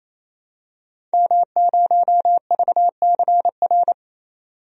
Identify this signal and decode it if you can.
Morse code, heard as a keyed tone, about 29 wpm, 690 Hz. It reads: M0VCR
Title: DXCC-100 Call Signs - 5 Characters - 28wpm